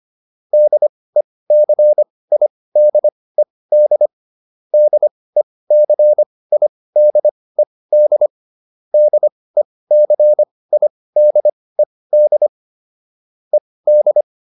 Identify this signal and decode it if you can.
Morse code, heard as a keyed tone, about 25 wpm, spaced as for 18 wpm, 600 Hz. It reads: DECIDED DECIDED DECIDED ED